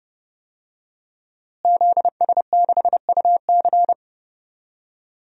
Morse code, heard as a keyed tone, about 31 wpm, 700 Hz. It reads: ZS6UC
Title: DXCC-100 Call Signs - 5 Characters - Repeated 2x - 30wpm